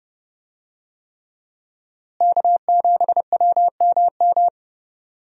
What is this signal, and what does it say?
Morse code, heard as a keyed tone, about 30 wpm, 700 Hz. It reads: K7WMM
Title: USA Call Signs - 5 Characters - Repeated 3x - 30wpm